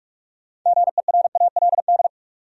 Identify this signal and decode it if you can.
Morse code, heard as a keyed tone, about 42 wpm, 700 Hz. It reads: GERALD
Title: USA Common Names - 45wpm